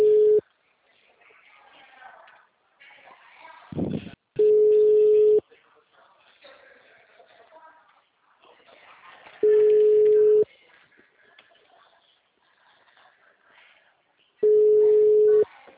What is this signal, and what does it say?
Ringback tone (Europe, 425 Hz), with speech.